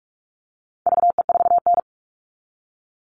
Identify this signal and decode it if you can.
Morse code, heard as a keyed tone, about 44 wpm, 700 Hz. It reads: VE4N
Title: All Cut Numbers - 4-Digit 45wpm